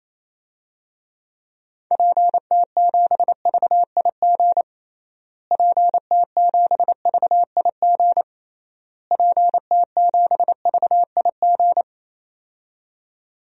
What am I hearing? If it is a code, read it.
Morse code, heard as a keyed tone, about 28 wpm, 700 Hz. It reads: PT7VIG PT7VIG PT7VIG
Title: DXCC-100 Call Signs - 6 Characters - Repeated 3x - 28wpm